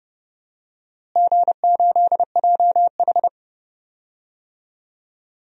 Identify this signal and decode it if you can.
Morse code, heard as a keyed tone, about 32 wpm, 700 Hz. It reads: G8JH
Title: DXCC-100 Call Signs - 4 Characters - Repeated 3x - 30wpm